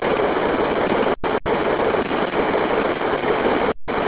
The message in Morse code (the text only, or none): none